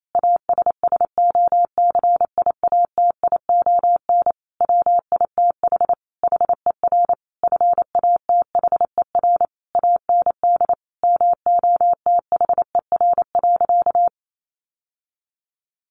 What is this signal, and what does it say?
Morse code, heard as a keyed tone, about 28 wpm, 700 Hz. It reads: ASSOCIATION WITH HER FATHER AND MOTHER.
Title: Sentences from Top 1100 Words 28wpm